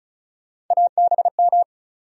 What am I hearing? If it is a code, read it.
Morse code, heard as a keyed tone, about 34 wpm, 700 Hz. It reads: ABM
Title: USA Call Sign Postfixes - 3 Letters 35wpm